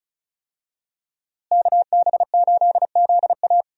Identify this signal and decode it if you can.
Morse code, heard as a keyed tone, about 36 wpm, 700 Hz. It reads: KB8ZA
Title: USA Call Signs - 5 Characters - Repeated 3x - 35wpm